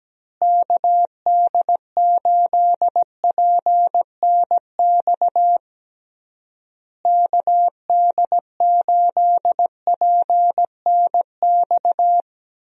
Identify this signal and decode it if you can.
Morse code, heard as a keyed tone, about 17 wpm, 700 Hz. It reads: KD8PNX KD8PNX
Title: USA Call Signs - 6 Characters - Repeated 3x - 17wpm